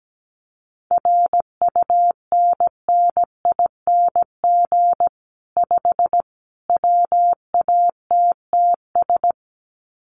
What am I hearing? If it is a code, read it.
Morse code, heard as a keyed tone, about 17 wpm, 700 Hz. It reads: RUNNING 5 WATTS
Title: Common QSO Phrases 17wpm